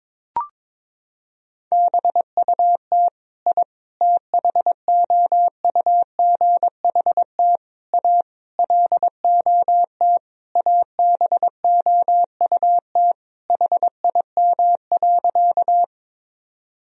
Morse code, heard as a keyed tone, about 22 wpm, 700 Hz. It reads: BUT I THOUGHT A LOT ABOUT HIM.
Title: Sentences from Top 1000 Words - Review Part 3 - 22wpm